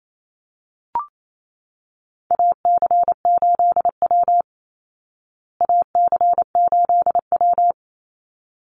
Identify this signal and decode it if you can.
Morse code, heard as a keyed tone, about 28 wpm, 700 Hz. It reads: AC8W AC8W
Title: Call Signs Repeated 3x - 28wpm